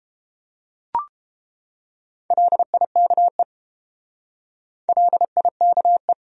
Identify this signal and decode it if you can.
Morse code, heard as a keyed tone, about 30 wpm, 700 Hz. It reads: LIKE LIKE